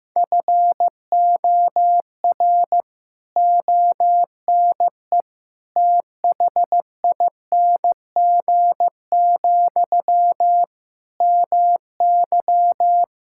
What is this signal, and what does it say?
Morse code, heard as a keyed tone, about 14 wpm, 700 Hz. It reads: FOR ONE THING, MY
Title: Sentences from Top 1000 Words - Review Part 6 - 15wpm